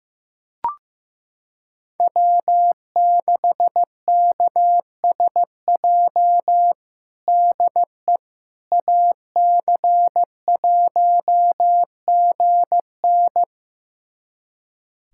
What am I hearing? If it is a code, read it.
Morse code, heard as a keyed tone, about 15 wpm, 700 Hz. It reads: W6KSJ DE AC1GN
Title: USA Callsign DE Callsign - 5 Characters - 15wpm